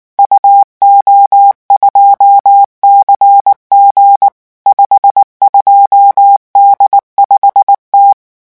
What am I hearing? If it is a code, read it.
Morse code, heard as a keyed tone, about 19 wpm, 800 Hz. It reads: UO2CG 52D5T
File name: morse mp3 / revision14-vitesse 18 mots